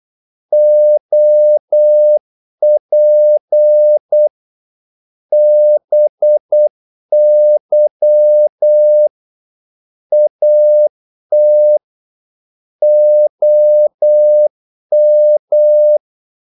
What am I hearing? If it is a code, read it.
Morse code, heard as a keyed tone, about 8 wpm, 600 Hz. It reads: OP BY AT OM
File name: Korte ord DK 8wpm